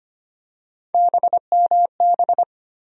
Morse code, heard as a keyed tone, about 25 wpm, 700 Hz. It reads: BMB
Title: USA Call Sign Postfixes - 3 Letters 25wpm